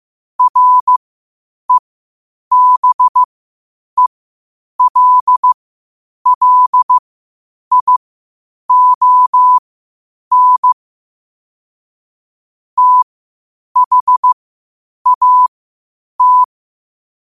Morse code, heard as a keyed tone, about 15 wpm, spaced as for 8 wpm, 1000 Hz. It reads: REBELLION THAT
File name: qotdfiles / 8wpm